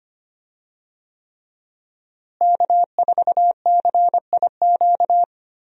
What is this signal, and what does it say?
Morse code, heard as a keyed tone, about 25 wpm, 700 Hz. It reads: K4CIQ
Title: USA Call Signs - 5 Characters - Repeated 3x - 25wpm